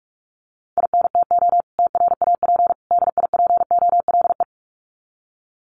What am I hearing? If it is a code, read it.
Morse code, heard as a keyed tone, about 44 wpm, 700 Hz. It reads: INTO TRAP DIPOLE
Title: Common QSO Phrases 45wpm